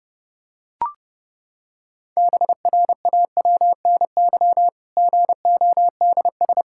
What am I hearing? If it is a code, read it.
Morse code, heard as a keyed tone, about 32 wpm, 700 Hz. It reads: BRAWNY GODS
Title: Pangram Challenge v2 - 30wpm